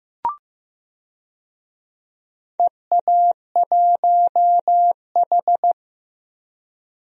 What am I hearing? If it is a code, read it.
Morse code, heard as a keyed tone, about 15 wpm, 700 Hz. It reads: EA1H
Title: DXCC-100 Call Signs - 4 Characters - 15wpm